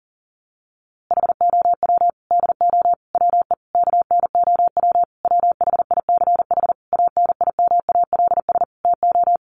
Morse code, heard as a keyed tone, about 40 wpm, 700 Hz. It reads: HOW DO WE KNOW WHICH ANIMALS TO
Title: A Sound Of Thunder 40wpm